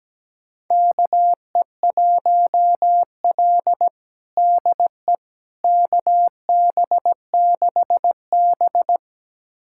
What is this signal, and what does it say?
Morse code, heard as a keyed tone, about 17 wpm, 700 Hz. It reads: KE1L DE KB6B
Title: USA Callsign DE Callsign - 4 Characters - 17wpm